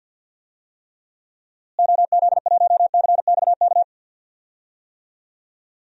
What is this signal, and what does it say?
Morse code, heard as a keyed tone, about 51 wpm, 700 Hz. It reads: OZ1CXK
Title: DXCC-100 Call Signs - 6 Characters - Repeated 3x - 50wpm